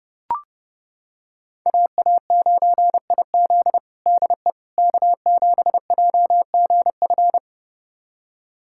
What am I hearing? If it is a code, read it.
Morse code, heard as a keyed tone, about 32 wpm, 700 Hz. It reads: AA9IZ DE K7JGF
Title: USA Callsign DE Callsign - 5 Characters - 30wpm